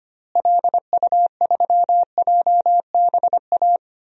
Morse code, heard as a keyed tone, about 25 wpm, 700 Hz. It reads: LU3JBA
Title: DXCC-100 Call Signs - 6 Characters - Repeated 2x - 25wpm